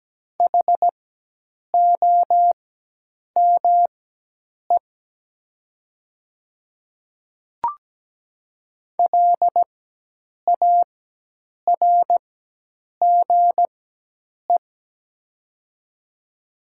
Morse code, heard as a keyed tone, about 17 wpm, spaced as for 8 wpm, 700 Hz. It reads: HOME LARGE
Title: 4x Character Spacing - Top 500 Words - 17wpm